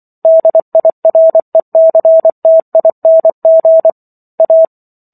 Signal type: Morse code